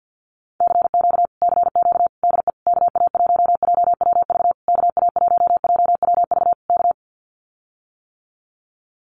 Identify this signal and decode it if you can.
Morse code, heard as a keyed tone, about 48 wpm, 700 Hz. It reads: CQ CQ DE KA1JWV KA1JWV K
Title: Common QSO Phrases Encore 50wpm